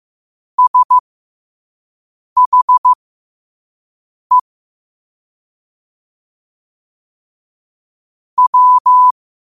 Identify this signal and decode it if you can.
Morse code, heard as a keyed tone, about 15 wpm, spaced as for 5 wpm, 1000 Hz. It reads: SHE W
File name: qotdfiles / 5wpm